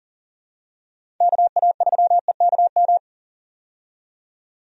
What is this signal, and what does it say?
Morse code, heard as a keyed tone, about 40 wpm, 700 Hz. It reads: KA3EKM
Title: USA Call Signs - 6 Characters - 40wpm